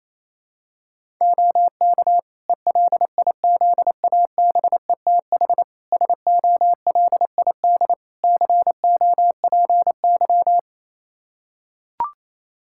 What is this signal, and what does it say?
Morse code, heard as a keyed tone, about 28 wpm, 700 Hz. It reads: OK ELIZABETH SOLID COPY
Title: Practice QSO - Exchange 3 - More Info - 28wpm